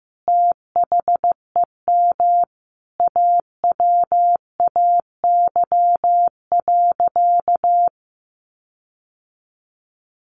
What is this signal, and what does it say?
Morse code, heard as a keyed tone, about 15 wpm, 700 Hz. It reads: THEM AWAY.
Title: Sentences from Top 1500 Words 15wpm